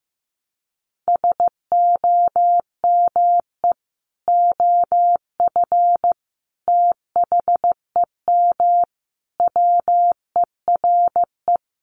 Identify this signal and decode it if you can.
Morse code, heard as a keyed tone, about 15 wpm, 700 Hz. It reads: SOME OF THEM WERE
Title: Sentences from Top 1200 Words 15wpm